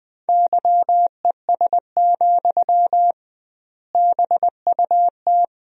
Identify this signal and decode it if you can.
Morse code, heard as a keyed tone, about 21 wpm, 700 Hz. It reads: YES, BUT
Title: 2x Word-Spacing - Sentences from Top 500 Words - 20wpm